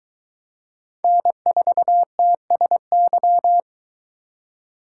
Morse code, heard as a keyed tone, about 20 wpm, 700 Hz. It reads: N4TSY